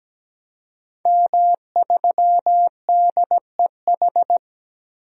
Morse code, heard as a keyed tone, about 17 wpm, 700 Hz. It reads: M3DEH
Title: DXCC-100 Call Signs - 5 Characters - Repeated 2x - 17wpm